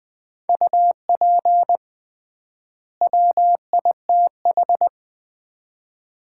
Morse code, heard as a keyed tone, about 20 wpm, 690 Hz. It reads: UP WITH
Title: 3x Word-Spacing - Sets of 2 Words - 20wpm